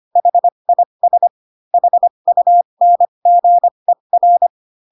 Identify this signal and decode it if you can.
Morse code, heard as a keyed tone, about 25 wpm, spaced as for 22 wpm, 700 Hz. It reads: HIS HUNGER